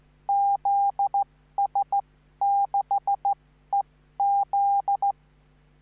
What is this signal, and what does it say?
Morse code, heard as a keyed tone, about 14 wpm, spaced as for 12 wpm, 800 Hz. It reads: ZS6EZ
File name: CQvonZS6EZ